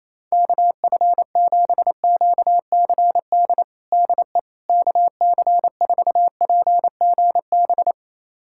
Morse code, heard as a keyed tone, about 28 wpm, 700 Hz. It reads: KF7QCD DE KC4PGB
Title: USA Callsign DE Callsign 28wpm